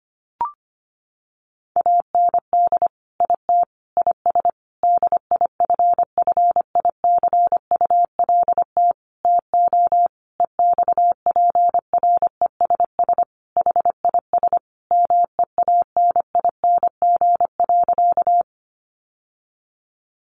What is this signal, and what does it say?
Morse code, heard as a keyed tone, about 25 wpm, 700 Hz. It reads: AND IT IS DIFFICULT TO EXPRESS HIS MEANING.
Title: Sentences from Top 1100 Words 25wpm